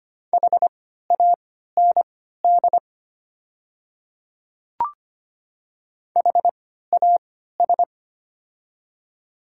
Morse code, heard as a keyed tone, about 25 wpm, spaced as for 14 wpm, 690 Hz. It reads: HAND HAS
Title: 3x Character Spacing - Top 300 Words - 25wpm